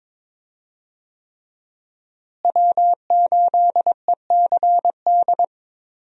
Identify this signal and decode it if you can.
Morse code, heard as a keyed tone, about 22 wpm, 700 Hz. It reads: W8ECD
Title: USA Call Signs - 5 Characters - Repeated 3x - 22wpm